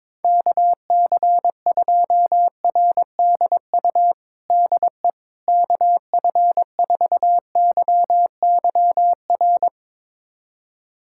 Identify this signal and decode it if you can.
Morse code, heard as a keyed tone, about 22 wpm, 700 Hz. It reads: KC2RDU DE KF4YYR
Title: USA Callsign DE Callsign 22wpm